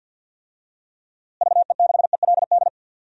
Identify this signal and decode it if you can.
Morse code, heard as a keyed tone, about 47 wpm, 700 Hz. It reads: VE6ELD
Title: DXCC-100 Call Signs - 6 Characters - Repeated 3x - 50wpm